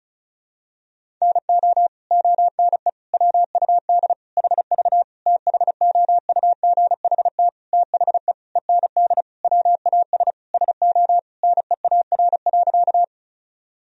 Morse code, heard as a keyed tone, about 35 wpm, 700 Hz. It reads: NO ONE WUD HV THOUGHT THE END WAS SO NEAR.
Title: Sentences from Top 300 Words and Abbreviations 35wpm